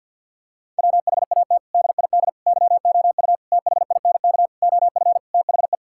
Morse code, heard as a keyed tone, about 50 wpm, 700 Hz. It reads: WHAT DID YOU THINK OF THE
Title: Sentences from Top 1000 Words - Review Part 5 - 50wpm